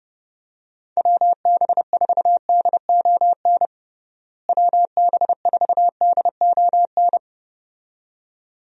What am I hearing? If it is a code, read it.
Morse code, heard as a keyed tone, about 29 wpm, 700 Hz. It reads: WB4DON WB4DON
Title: USA Call Signs - 6 Characters - Repeated 2x - 30wpm